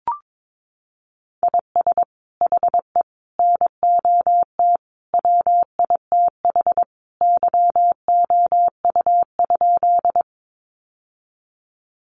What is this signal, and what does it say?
Morse code, heard as a keyed tone, about 22 wpm, 700 Hz. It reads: IS HE NOT WITH YOU?
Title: Sentences from Top 600 Words 22wpm